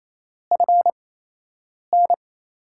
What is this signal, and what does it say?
Morse code, heard as a keyed tone, about 28 wpm, spaced as for 8 wpm, 700 Hz. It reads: FN
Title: USA Call Signs - 5 Characters - 8x Character Spacing - 28wpm